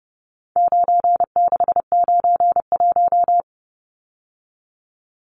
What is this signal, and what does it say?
Morse code, heard as a keyed tone, about 30 wpm, 700 Hz. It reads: 9691
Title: Numbers - 4-Digit 30wpm